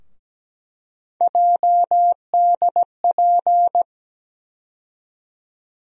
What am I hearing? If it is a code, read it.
Morse code, heard as a keyed tone, about 17 wpm, 700 Hz. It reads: JDP
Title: USA Call Sign Postfixes - 3 Letters 17wpm